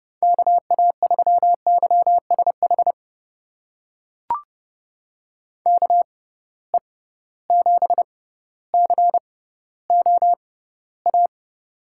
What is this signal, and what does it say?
Morse code, heard as a keyed tone, about 32 wpm, spaced as for 16 wpm, 700 Hz. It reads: KA3YSH KE7COA
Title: USA Call Signs - 6 Characters - 6x Character Spacing - 30wpm